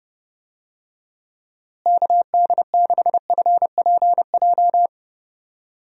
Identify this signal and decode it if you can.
Morse code, heard as a keyed tone, about 30 wpm, 700 Hz. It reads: KD6FPJ
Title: USA Call Signs - 6 Characters - Repeated 3x - 30wpm